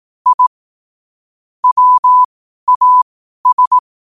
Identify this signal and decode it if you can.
Morse code, heard as a keyed tone, about 18 wpm, spaced as for 12 wpm, 1000 Hz. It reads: I WAS
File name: qotdfiles / 12wpm